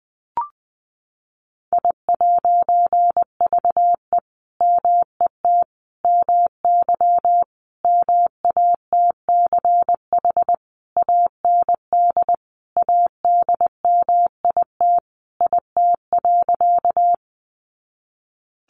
Morse code, heard as a keyed tone, about 20 wpm, 700 Hz. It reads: I'VE MET MY MATCH AND ADMIT IT.
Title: A Sound Of Thunder 20wpm